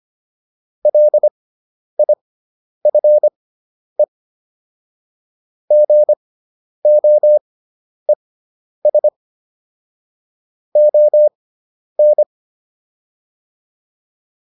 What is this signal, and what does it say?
Morse code, heard as a keyed tone, about 25 wpm, spaced as for 10 wpm, 600 Hz. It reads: LIFE GOES ON